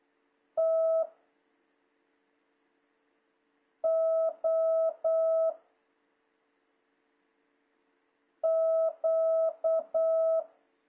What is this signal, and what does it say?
Morse code, heard as a keyed tone, about 8 wpm, 660 Hz. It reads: T O Q
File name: jh8cbh / 2026cw